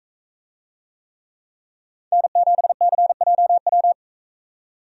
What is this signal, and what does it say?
Morse code, heard as a keyed tone, about 42 wpm, 700 Hz. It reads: N7CJW